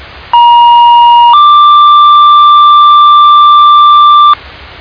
Two-tone page, heard as a one-second tone, then a three-second tone, about 950 then 1180 Hz, with noise. It is heard from 0.3 to 4.3 s.